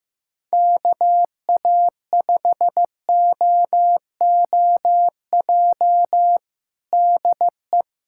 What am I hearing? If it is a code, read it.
Morse code, heard as a keyed tone, about 15 wpm, 700 Hz. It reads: KA5OOJ DE
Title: USA Callsign DE Callsign 15wpm